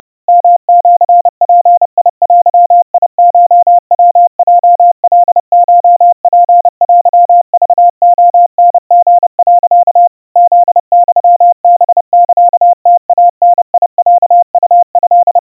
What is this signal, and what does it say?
Morse code (Japanese Wabun code), heard as a keyed tone, about 30 wpm, 700 Hz. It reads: ヨシヅデコヤヲカコツテクレタリ、 フユハサムイダロウト
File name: kawa / Morse sample / kamyu 110 cpm 700 Hz Crispy